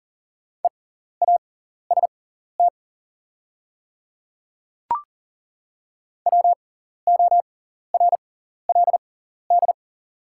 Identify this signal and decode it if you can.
Morse code, heard as a keyed tone, about 40 wpm, spaced as for 14 wpm, 700 Hz. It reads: EAST WORLD
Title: 6x Character Spacing - Top 500 Words - 40wpm